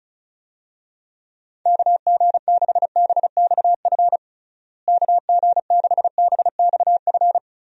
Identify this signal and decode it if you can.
Morse code, heard as a keyed tone, about 34 wpm, 700 Hz. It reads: KG6BXF KG6BXF
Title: USA Call Signs - 6 Characters - Repeated 3x - 35wpm